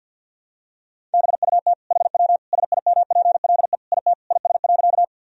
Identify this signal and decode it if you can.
Morse code, heard as a keyed tone, about 48 wpm, 700 Hz. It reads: BUT HW SIMPLE IT IS.
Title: Sentences from Top 400 Words and Abbreviations 50wpm